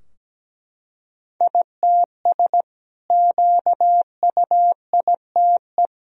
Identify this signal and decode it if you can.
Morse code, heard as a keyed tone, about 17 wpm, 700 Hz. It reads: ITS QUITE
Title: Sentences from Top 1000 Words - Review Part 3 - 17wpm